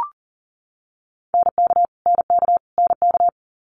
Morse code, heard as a keyed tone, about 40 wpm, 690 Hz. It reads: NK NK NK
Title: Multi-letter Phonemes Repeated 3x 40wpm